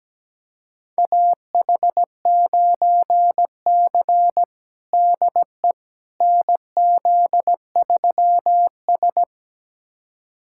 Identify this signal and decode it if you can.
Morse code, heard as a keyed tone, about 17 wpm, 700 Hz. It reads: AH9C DE NZ3S
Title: USA Callsign DE Callsign - 4 Characters - 17wpm